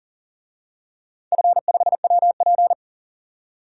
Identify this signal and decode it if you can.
Morse code, heard as a keyed tone, about 39 wpm, 700 Hz. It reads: F5WP
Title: DXCC-100 Call Signs - 4 Characters - 40wpm